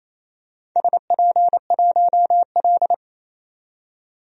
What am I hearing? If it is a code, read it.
Morse code, heard as a keyed tone, about 27 wpm, 690 Hz. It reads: SP1L
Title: DXCC-100 Call Signs - 4 Characters - 28wpm